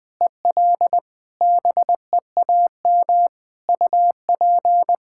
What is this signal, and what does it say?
Morse code, heard as a keyed tone, about 20 wpm, 700 Hz. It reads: EL BEAM UP
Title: Practice QSO - Exchange 2 - Info - 20wpm